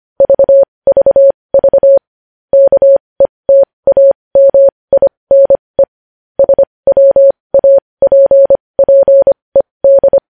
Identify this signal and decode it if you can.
Morse code, heard as a keyed tone, about 25 wpm, spaced as for 20 wpm, 550 Hz. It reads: VVV KETAMINE SWAPPED
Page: News Headlines in Morse Code at 20 WPM